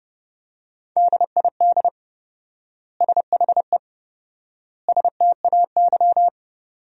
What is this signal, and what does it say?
Morse code, heard as a keyed tone, about 32 wpm, 700 Hz. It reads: DID SHE STAY
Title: 4x Word-Spacing - Sentences from Top 500 Words - 30wpm